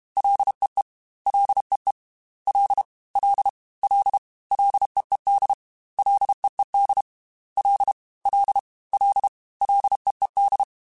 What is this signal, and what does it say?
Morse code, heard as a keyed tone, about 32 wpm, 800 Hz. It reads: LEE LEE L L L LEED LEED L L L LEED
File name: Ls / Single L m3s / Single L @30